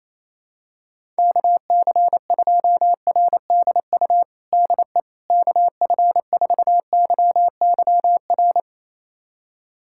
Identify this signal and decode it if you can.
Morse code, heard as a keyed tone, about 28 wpm, 700 Hz. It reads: KC2RDU DE KF4YYR
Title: USA Callsign DE Callsign 28wpm